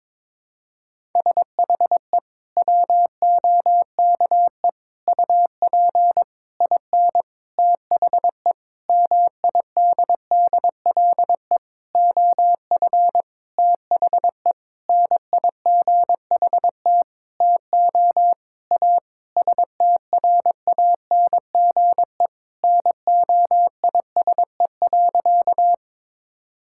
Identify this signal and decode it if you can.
Morse code, heard as a keyed tone, about 22 wpm, 700 Hz. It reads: SHE WOKE UP IN THE MIDDLE OF THE NIGHT TO A STRANGE NOISE.
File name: practice / 22wpm / Sentences with 8-Word Phrases - 22wpm